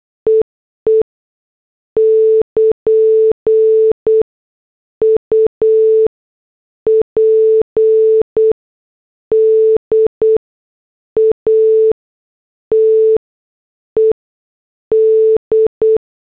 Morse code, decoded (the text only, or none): EE <KN> UPDATED